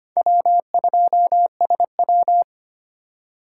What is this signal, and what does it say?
Morse code, heard as a keyed tone, about 25 wpm, 700 Hz. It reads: W2SW
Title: USA Call Signs - 4 Characters - 8x Character Spacing - 25wpm